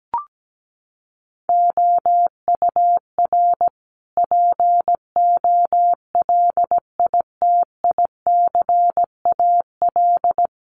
Morse code, heard as a keyed tone, about 17 wpm, 700 Hz. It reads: OUR POLITICAL
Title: Sentences from Top 900 Words 17wpm